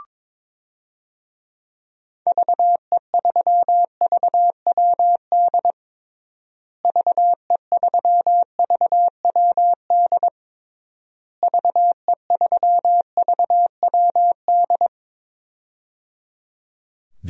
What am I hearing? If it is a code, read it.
Morse code, heard as a keyed tone, about 22 wpm, 700 Hz. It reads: VE3VWD VE3VWD VE3VWD
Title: DXCC-100 Call Signs - 6 Characters - Repeated 3x - 22wpm